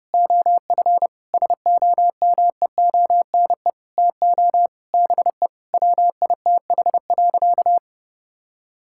Morse code, heard as a keyed tone, about 30 wpm, 700 Hz. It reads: OF SOMEONE TO BE WITH.
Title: Sentences from Top 1000 Words - Review Part 5 - 30wpm